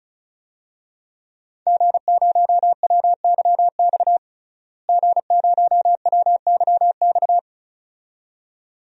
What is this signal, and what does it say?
Morse code, heard as a keyed tone, about 36 wpm, 700 Hz. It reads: G0WYX G0WYX
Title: DXCC-100 Call Signs - 5 Characters - Repeated 2x - 35wpm